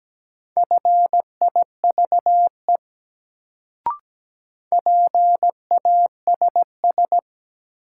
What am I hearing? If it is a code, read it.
Morse code, heard as a keyed tone, about 17 wpm, 700 Hz. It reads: FIVE PASS
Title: Sentences from Top 400 Words - New Word Drill - 17wpm